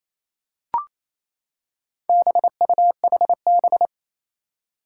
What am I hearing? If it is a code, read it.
Morse code, heard as a keyed tone, about 29 wpm, 700 Hz. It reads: BUHB
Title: Four Challenging Letter-Number Combinations 28wpm